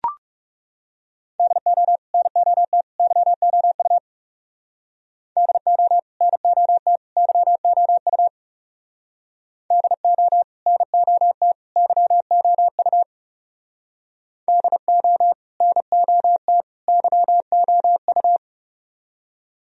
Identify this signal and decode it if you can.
Morse code, heard as a keyed tone, about 35 wpm, 700 Hz. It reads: DO NOT YOU DO NOT YOU DO NOT YOU DO NOT YOU
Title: Speed-Racing - Sets of 3 Words 30wpm